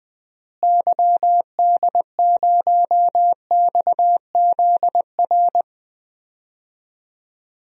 Morse code, heard as a keyed tone, about 20 wpm, 700 Hz. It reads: YD0XZR
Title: DXCC-100 Call Signs - 6 Characters - Repeated 2x - 20wpm